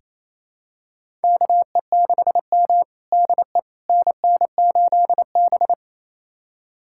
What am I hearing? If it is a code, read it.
Morse code, heard as a keyed tone, about 28 wpm, 700 Hz. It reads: KE6M DE NN8B
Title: USA Callsign DE Callsign - 4 Characters - 28wpm